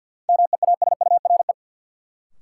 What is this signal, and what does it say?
Morse code, heard as a keyed tone, about 53 wpm, 700 Hz. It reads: MEASURE
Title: Sentences from Top 1200 Words - New Word Drill Alt - 50wpm